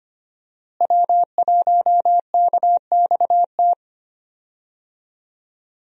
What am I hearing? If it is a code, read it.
Morse code, heard as a keyed tone, about 25 wpm, 700 Hz. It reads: W1KXT
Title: USA Call Signs - 5 Characters - Repeated 2x - 25wpm